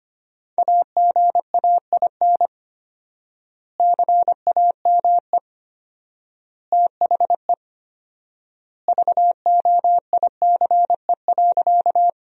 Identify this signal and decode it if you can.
Morse code, heard as a keyed tone, about 25 wpm, 700 Hz. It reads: AGAIN CAME THE VOICE.
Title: 4x Word-Spacing - Sentences from Top 500 Words - 25wpm